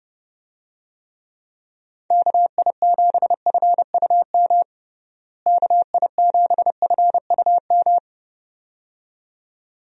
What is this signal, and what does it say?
Morse code, heard as a keyed tone, about 30 wpm, 700 Hz. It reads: KI7FUM KI7FUM
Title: USA Call Signs - 6 Characters - Repeated 2x - 30wpm